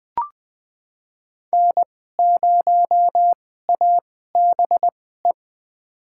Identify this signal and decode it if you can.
Morse code, heard as a keyed tone, about 20 wpm, spaced as for 15 wpm, 700 Hz. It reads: N0ABE
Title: USA Call Signs - 5 Characters - 2x Character Spacing - 20wpm